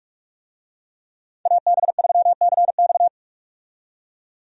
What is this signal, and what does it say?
Morse code, heard as a keyed tone, about 46 wpm, 700 Hz. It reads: AB3CX
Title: USA Call Signs - 5 Characters - 45wpm